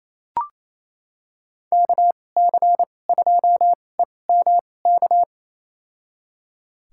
Morse code, heard as a keyed tone, about 28 wpm, spaced as for 20 wpm, 700 Hz. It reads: KC2EMK
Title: USA Call Signs - 6 Characters - 2x Character Spacing - 28wpm